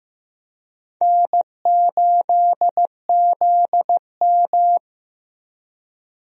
Morse code, heard as a keyed tone, about 15 wpm, 700 Hz. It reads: N8ZM
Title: USA Call Signs - 4 Characters - Repeated 3x - 15wpm